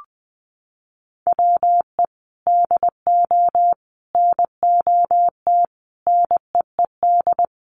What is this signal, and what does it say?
Morse code, heard as a keyed tone, about 20 wpm, 700 Hz. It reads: WE DO NOT NEED
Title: Sets of 5 Words Encore 20wpm